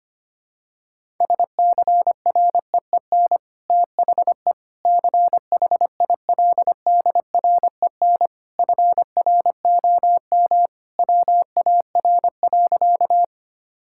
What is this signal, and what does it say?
Morse code, heard as a keyed tone, about 25 wpm, 700 Hz. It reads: SCREEN THE CHILDREN FROM WAR.
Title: Sentences from Top 1000 Words - Review Part 6 - 25wpm